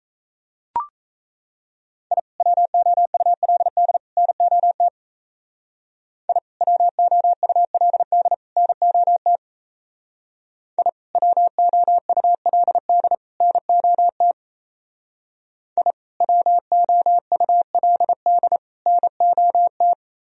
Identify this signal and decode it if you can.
Morse code, heard as a keyed tone, about 36 wpm, 700 Hz. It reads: I WOULD NOT I WOULD NOT I WOULD NOT I WOULD NOT